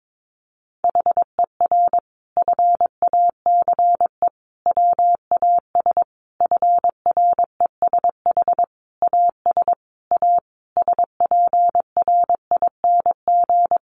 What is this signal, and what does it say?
Morse code, heard as a keyed tone, about 22 wpm, 700 Hz. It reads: HER FACE WAS FRESH AS A SPRING
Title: Sentences from Top 1200 Words 22wpm